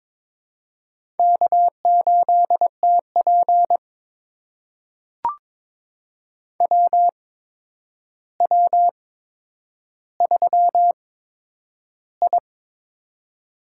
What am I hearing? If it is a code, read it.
Morse code, heard as a keyed tone, about 22 wpm, spaced as for 9 wpm, 700 Hz. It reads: K8TP WW3I
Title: USA Call Signs - 4 Characters - 8x Character Spacing - 22wpm